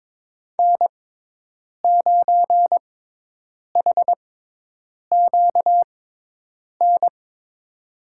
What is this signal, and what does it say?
Morse code, heard as a keyed tone, about 22 wpm, spaced as for 8 wpm, 700 Hz. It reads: N9HQN
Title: USA Call Signs - 5 Characters - 6x Character Spacing - 22wpm